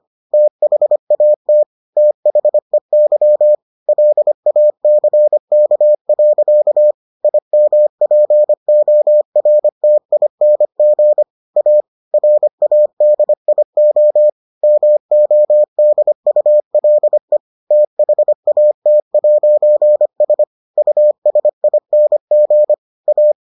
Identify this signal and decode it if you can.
Morse code, heard as a keyed tone, about 25 wpm, 600 Hz. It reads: THAT THEY LACK. IMPORTING A RADIO MODULE THAT'S USING A